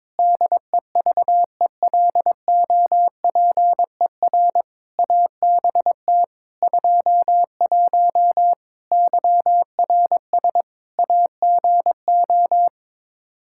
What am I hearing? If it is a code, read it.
Morse code, heard as a keyed tone, about 21 wpm, 700 Hz. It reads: DEVELOPER ABT 21 YRS AGO
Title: Practice QSO - Exchange 3 - More Info - 22wpm